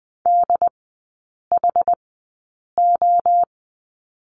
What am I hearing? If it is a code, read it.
Morse code, heard as a keyed tone, about 20 wpm, spaced as for 8 wpm, 700 Hz. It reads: DHO
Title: 067 Lesson 12e - Letter d - Three Characters 20wpm